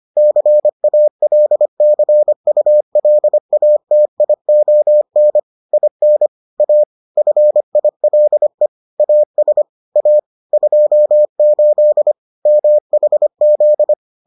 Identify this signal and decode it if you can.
Morse code, heard as a keyed tone, about 25 wpm, 600 Hz. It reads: CALCULATION IN A FILE AS A 28 MHZ